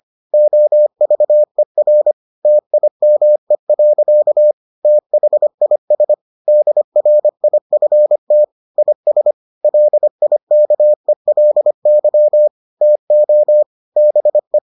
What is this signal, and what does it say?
Morse code, heard as a keyed tone, about 25 wpm, 600 Hz. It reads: OVER TIME. THIS DRIFT IS LIKELY TO BE